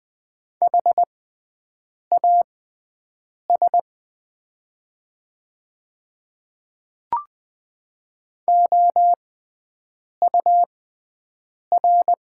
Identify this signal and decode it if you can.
Morse code, heard as a keyed tone, about 20 wpm, spaced as for 7 wpm, 700 Hz. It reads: HAS OUR
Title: 6x Character Spacing - Top 200 Words - 20wpm